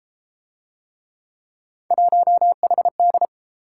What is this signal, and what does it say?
Morse code, heard as a keyed tone, about 32 wpm, 700 Hz. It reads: 1HD